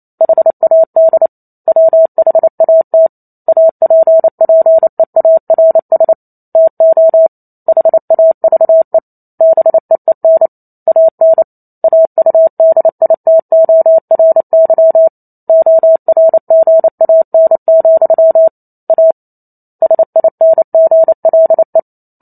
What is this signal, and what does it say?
Morse code, heard as a keyed tone, about 28 wpm, 650 Hz. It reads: HAD WHAT APPEARS TO HAVE BEEN AN AUDITORY ORGAN, A SINGLE